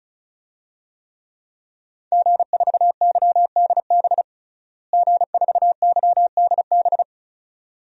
Morse code, heard as a keyed tone, about 36 wpm, 700 Hz. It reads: G4YDB G4YDB
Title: DXCC-100 Call Signs - 5 Characters - Repeated 2x - 35wpm